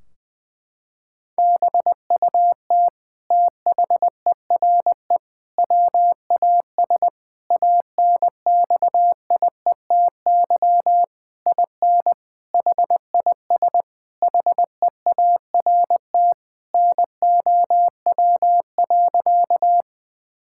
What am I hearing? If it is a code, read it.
Morse code, heard as a keyed tone, about 20 wpm, 700 Hz. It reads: BUT THERE WAS ANXIETY IN HIS HEART NOW.
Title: Sentences from Top 1600 Words 20wpm